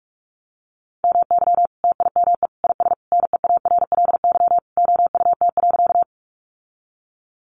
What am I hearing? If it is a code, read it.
Morse code, heard as a keyed tone, about 44 wpm, 700 Hz. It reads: MY TIME IS NEARLY OUT.
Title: Sentences from Top 1300 Words 45wpm